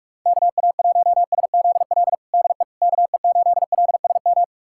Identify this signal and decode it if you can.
Morse code, heard as a keyed tone, about 46 wpm, 700 Hz. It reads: KA1SZL DE KE8LSM
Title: USA Callsign DE Callsign - 6 Characters - 45wpm